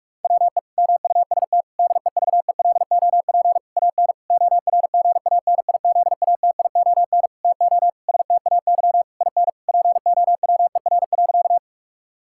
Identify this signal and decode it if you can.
Morse code, heard as a keyed tone, about 46 wpm, 700 Hz. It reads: WE MUST DEVELOP AN ORGANIZATION TO STAY IN POWER.
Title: Sentences from Top 1000 Words - Review Part 5 - 45wpm